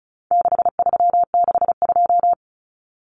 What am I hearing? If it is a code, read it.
Morse code, heard as a keyed tone, about 34 wpm, 700 Hz. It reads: 6362
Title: Numbers - 4-Digit 35wpm